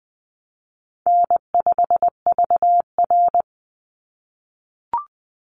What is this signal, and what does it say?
Morse code, heard as a keyed tone, about 20 wpm, 700 Hz. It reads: N5VR
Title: Easy Call Signs - 20wpm